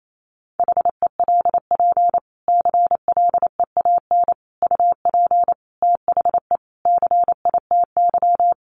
Morse code, heard as a keyed tone, about 28 wpm, 700 Hz. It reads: HELP CLEAN UP THE CITY
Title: Sentences from Top 1100 Words 28wpm